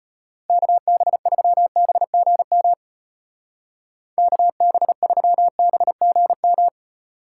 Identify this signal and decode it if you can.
Morse code, heard as a keyed tone, about 31 wpm, 700 Hz. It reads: KB3BGM KB3BGM
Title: Speed Racing - USA 2x3 Call Signs 25wpm